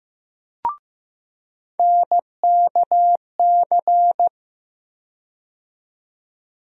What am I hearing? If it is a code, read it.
Morse code, heard as a keyed tone, about 15 wpm, 700 Hz. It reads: NKC
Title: USA Call Sign Postfixes - 3 Letters 15wpm